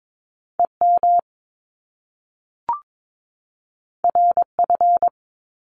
Morse code, heard as a keyed tone, about 22 wpm, 700 Hz. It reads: EM RF
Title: Top Two Letter Combinations in Words 22wpm